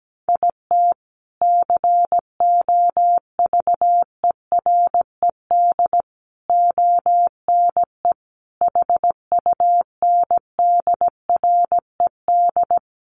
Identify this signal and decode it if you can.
Morse code, heard as a keyed tone, about 17 wpm, 700 Hz. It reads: IT COVERED ONE HUNDRED
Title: A Sound Of Thunder 17wpm